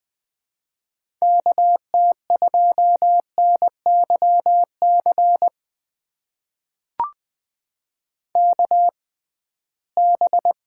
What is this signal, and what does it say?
Morse code, heard as a keyed tone, about 20 wpm, spaced as for 15 wpm, 700 Hz. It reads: KT2NYC KB
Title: USA Call Signs - 6 Characters - 6x Character Spacing - 20wpm